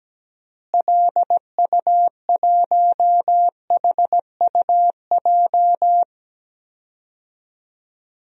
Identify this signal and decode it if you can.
Morse code, heard as a keyed tone, about 17 wpm, 700 Hz. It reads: LU1HUJ
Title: DXCC-100 Call Signs - 6 Characters - Repeated 3x - 17wpm